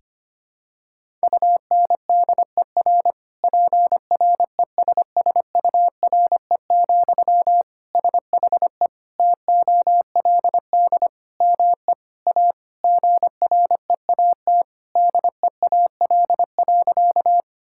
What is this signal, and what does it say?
Morse code, heard as a keyed tone, about 25 wpm, 700 Hz. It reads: UNDER PRESSURE, SHE TOLD ME A GREAT DEAL.
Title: Sentences from Top 1000 Words - Review Part 6 - 25wpm